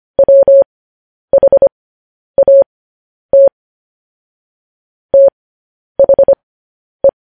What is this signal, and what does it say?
Morse code, heard as a keyed tone, about 25 wpm, spaced as for 10 wpm, 550 Hz. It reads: WHAT THE